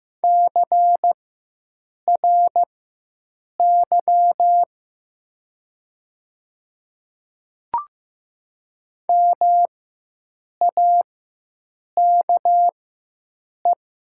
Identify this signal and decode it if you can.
Morse code, heard as a keyed tone, about 15 wpm, spaced as for 7 wpm, 700 Hz. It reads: CRY MAKE
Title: 4x Character Spacing - Top 500 Words - 15wpm